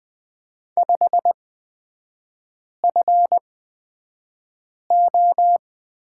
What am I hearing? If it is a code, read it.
Morse code, heard as a keyed tone, about 20 wpm, spaced as for 5 wpm, 700 Hz. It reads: 5FO
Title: 236 Lesson 38e - Letter q - Three Characters 20wpm